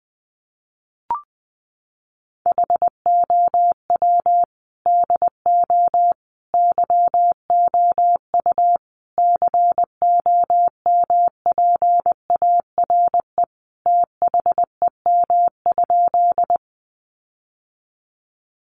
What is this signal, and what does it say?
Morse code, heard as a keyed tone, about 19 wpm, 700 Hz. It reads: HOW DO YOU COMPARE THEM?
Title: Sentences from Top 1300 Words 20wpm